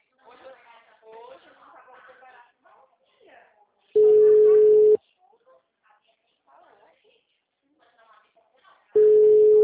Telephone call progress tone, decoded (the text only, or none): ringback tone